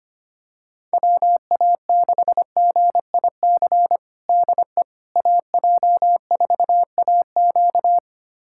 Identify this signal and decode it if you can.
Morse code, heard as a keyed tone, about 25 wpm, 700 Hz. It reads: WA6GIC DE AJ4AQ
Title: USA Callsign DE Callsign 25wpm